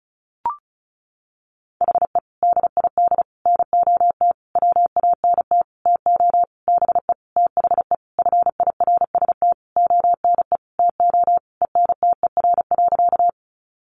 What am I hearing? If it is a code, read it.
Morse code, heard as a keyed tone, about 34 wpm, 700 Hz. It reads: HE DID NOT WANT TO BE THE FIRST ONE TO ENTER.
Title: Sentences from Top 1300 Words 35wpm